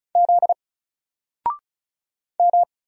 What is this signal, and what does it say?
Morse code, heard as a keyed tone, about 34 wpm, 700 Hz. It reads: Z M
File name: Single-Letter-Number-35wpm